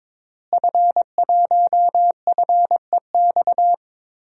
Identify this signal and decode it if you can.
Morse code, heard as a keyed tone, about 22 wpm, 700 Hz. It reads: F1FEX
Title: DXCC-100 Call Signs - 5 Characters - Repeated 2x - 22wpm